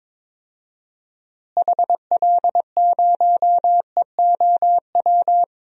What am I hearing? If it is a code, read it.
Morse code, heard as a keyed tone, about 22 wpm, 700 Hz. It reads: HL0EOW
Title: DXCC-100 Call Signs - 6 Characters - Repeated 2x - 22wpm